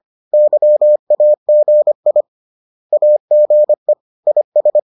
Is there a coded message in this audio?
Yes: Morse code from 0.3 to 4.8 s, heard as a keyed tone, about 25 wpm, 600 Hz. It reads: YAGI AGE IS